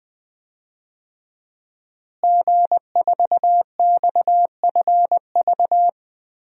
Morse code, heard as a keyed tone, about 20 wpm, 700 Hz. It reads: G4XFV
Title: DXCC-100 Call Signs - 5 Characters - Repeated 3x - 20wpm